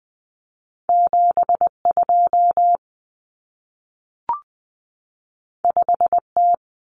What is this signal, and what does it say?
Morse code, heard as a keyed tone, about 20 wpm, 700 Hz. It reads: 72 5T
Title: Common Cut Numbers - 2-Digit 20wpm